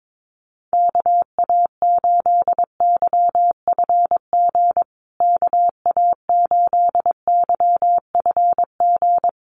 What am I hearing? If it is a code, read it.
Morse code, heard as a keyed tone, about 22 wpm, 700 Hz. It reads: KA8YFG KA8YFG
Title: USA Call Signs Repeated 2x 22wpm